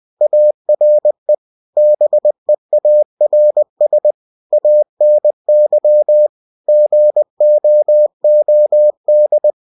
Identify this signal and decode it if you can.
Morse code, heard as a keyed tone, about 20 wpm, 600 Hz. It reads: ARE BEARS ANY GOOD
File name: cw-books / winnie-the-pooh / winnie-20wpm-page-0077